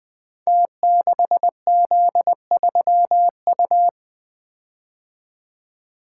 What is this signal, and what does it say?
Morse code, heard as a keyed tone, about 20 wpm, 700 Hz. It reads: T6Z3U
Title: Code Groups - 5 Letter:Numbers - 20wpm